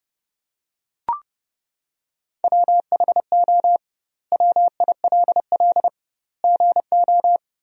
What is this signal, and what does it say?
Morse code, heard as a keyed tone, about 32 wpm, 700 Hz. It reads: WHO WILL GO
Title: 2x Word-Spacing - Sentences from Top 100 Words - 30wpm